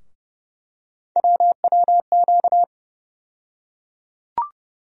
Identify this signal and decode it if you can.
Morse code, heard as a keyed tone, about 31 wpm, 700 Hz. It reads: WWQ
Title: USA Call Sign Postfixes - 3 Letters 30wpm